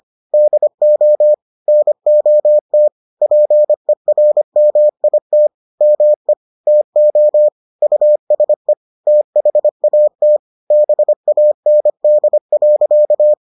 Morse code, heard as a keyed tone, about 25 wpm, 600 Hz. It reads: DO NOT PERMIT ME TO USE THAT BAND.